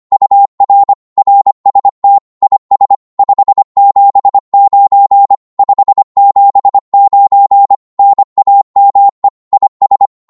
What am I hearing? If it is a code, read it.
Morse code, heard as a keyed tone, about 24 wpm, 800 Hz. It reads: UR RST IS 579 579 NAME IS
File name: Morse / qso-25wpm